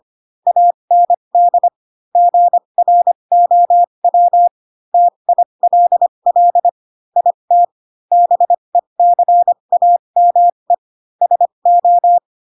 Morse code, heard as a keyed tone, about 25 wpm, spaced as for 22 wpm, 700 Hz. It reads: AND GROW TILL IT BECAME SO
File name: Pinocchio / Pinocchio ch3 18 25 ik1hge